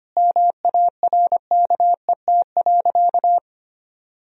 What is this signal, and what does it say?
Morse code, heard as a keyed tone, about 25 wpm, 700 Hz. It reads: MARKET.
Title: Sentences from Top 1000 Words - Review Part 3 - 25wpm